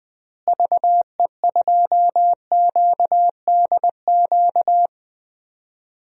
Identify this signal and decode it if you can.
Morse code, heard as a keyed tone, about 20 wpm, 700 Hz. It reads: VE2QDQ
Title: DXCC-100 Call Signs - 6 Characters - Repeated 2x - 20wpm